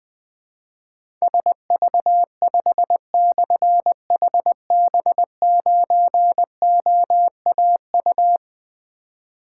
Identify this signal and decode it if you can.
Morse code, heard as a keyed tone, about 20 wpm, 700 Hz. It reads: SV5/HB9OAU
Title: Difficult Call Signs 20wpm